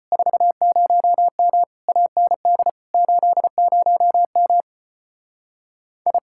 Morse code, heard as a keyed tone, about 33 wpm, 700 Hz. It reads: 40M AND 80M I